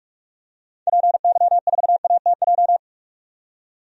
Morse code, heard as a keyed tone, about 46 wpm, 700 Hz. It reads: PY4ATJ
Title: DXCC-100 Call Signs - 6 Characters - 45wpm